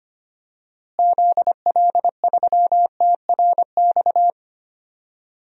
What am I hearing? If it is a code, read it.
Morse code, heard as a keyed tone, about 25 wpm, 700 Hz. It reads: ZL3TRX
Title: DXCC-100 Call Signs - 6 Characters - Repeated 3x - 25wpm